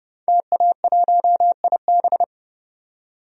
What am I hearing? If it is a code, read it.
Morse code, heard as a keyed tone, about 30 wpm, 700 Hz. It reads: TA1IB
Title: DXCC-100 Call Signs - 5 Characters - 30wpm